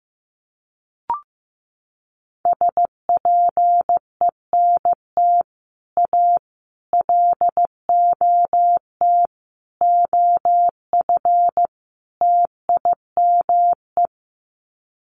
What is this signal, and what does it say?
Morse code, heard as a keyed tone, about 15 wpm, 700 Hz. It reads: SPENT A LOT OF TIME
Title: Sets of 5 Words 15wpm